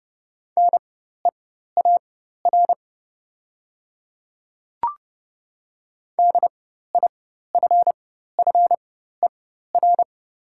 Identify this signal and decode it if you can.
Morse code, heard as a keyed tone, about 30 wpm, spaced as for 14 wpm, 700 Hz. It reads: NEAR DIFFER
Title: 4x Character Spacing - Top 300 Words - 30wpm